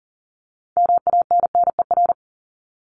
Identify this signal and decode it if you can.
Morse code, heard as a keyed tone, about 40 wpm, 700 Hz. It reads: MANNER
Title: Top 1501-1750 Words 40wpm